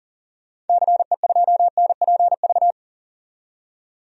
Morse code, heard as a keyed tone, about 40 wpm, 700 Hz. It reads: CE2NPV
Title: DXCC-100 Call Signs - 6 Characters - 40wpm